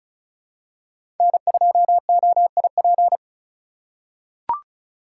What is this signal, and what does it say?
Morse code, heard as a keyed tone, about 34 wpm, 700 Hz. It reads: N2OIP
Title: USA Call Signs - 5 Characters - 6x Character Spacing - 35wpm